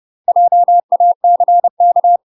Morse code, heard as a keyed tone, about 30 wpm, 700 Hz. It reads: JACK
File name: UK MASTER QSOs TRAINING / 30WPM / UK MASTER QSOs 0027